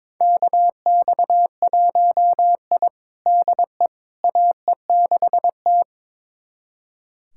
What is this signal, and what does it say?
Morse code, heard as a keyed tone, about 21 wpm, 700 Hz. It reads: KX1I DE AE6T
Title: USA Callsign DE Callsign - 4 Characters - 22wpm